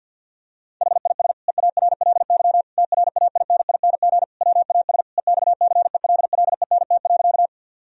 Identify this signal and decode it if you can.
Morse code, heard as a keyed tone, about 46 wpm, 700 Hz. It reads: HIS EARLY TRAINING WAS EXCELLENT.
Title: Sentences from Top 1000 Words - Review Part 4 - 50wpm